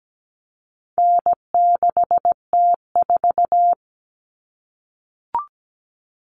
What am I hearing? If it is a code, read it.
Morse code, heard as a keyed tone, about 17 wpm, 700 Hz. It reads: N6T4
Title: All Cut Numbers - 4-Digit 17wpm